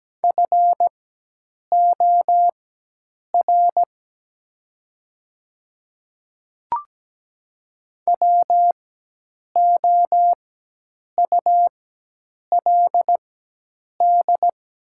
Morse code, heard as a keyed tone, about 17 wpm, spaced as for 8 wpm, 700 Hz. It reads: FOR WOULD
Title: 4x Character Spacing - Top 200 Words - 17wpm